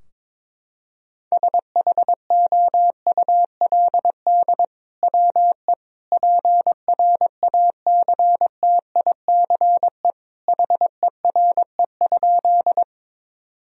Morse code, heard as a keyed tone, about 21 wpm, 700 Hz. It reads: SHOULD WE PRACTICE HERE?
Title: Sentences from Top 1000 Words - Review Part 4 - 22wpm